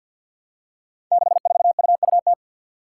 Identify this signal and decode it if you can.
Morse code, heard as a keyed tone, about 52 wpm, 700 Hz. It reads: 64UUT
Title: Five Challenging Letter-Number Combinations 50wpm